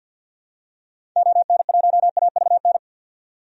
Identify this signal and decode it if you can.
Morse code, heard as a keyed tone, about 49 wpm, 700 Hz. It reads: ON1AVN
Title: DXCC-100 Call Signs - 6 Characters - 50wpm